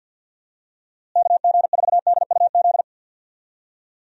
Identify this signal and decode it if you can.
Morse code, heard as a keyed tone, about 51 wpm, 700 Hz. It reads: KG4DUZ
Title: USA Call Signs - 6 Characters - Repeated 3x - 50wpm